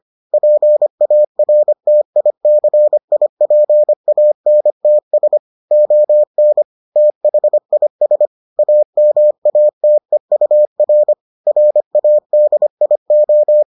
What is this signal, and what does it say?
Morse code, heard as a keyed tone, about 25 wpm, 600 Hz. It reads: PARTICIPANTS ON THIS AMATEUR RADIO